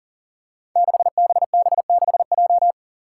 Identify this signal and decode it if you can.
Morse code, heard as a keyed tone, about 39 wpm, 700 Hz. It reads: 6BB6J
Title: Five Challenging Letter-Number Combinations 40wpm